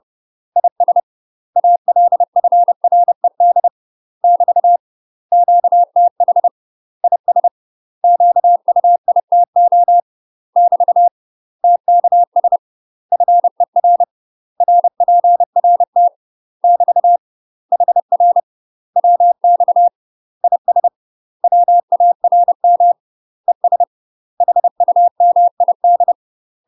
Morse code, heard as a keyed tone, about 30 wpm, 700 Hz. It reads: IS ALFRED <BT> QTH IS QUITO <BT> TKS FER RPRT <BT> HR WX IS WARM ES HUMID
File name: UK MASTER QSOs TRAINING / 30WPM / UK MASTER QSOs 0034